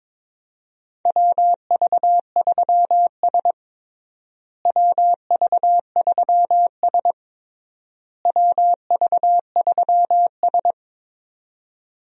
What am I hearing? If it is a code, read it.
Morse code, heard as a keyed tone, about 22 wpm, 700 Hz. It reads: WV3S WV3S WV3S
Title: USA Call Signs - 4 Characters - Repeated 3x - 22wpm